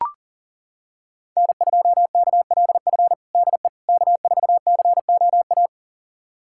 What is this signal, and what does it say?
Morse code, heard as a keyed tone, about 40 wpm, 700 Hz. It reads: N2KLF DE K4COA
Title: USA Callsign DE Callsign - 5 Characters - 40wpm